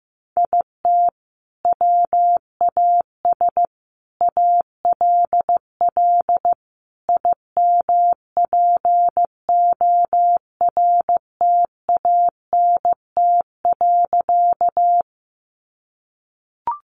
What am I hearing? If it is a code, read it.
Morse code, heard as a keyed tone, about 15 wpm, 700 Hz. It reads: IT WAS ALL IMPORTANT.
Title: Sentences from Top 600 Words 15wpm